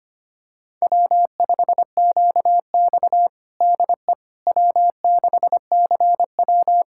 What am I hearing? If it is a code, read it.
Morse code, heard as a keyed tone, about 25 wpm, 700 Hz. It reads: W5QX DE W6CW
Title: USA Callsign DE Callsign - 4 Characters - 25wpm